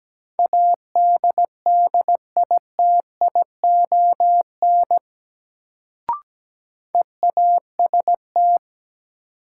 Morse code, heard as a keyed tone, about 17 wpm, 700 Hz. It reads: ADDITION EAST
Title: Sets of 3 Words - New Word Drill - 17wpm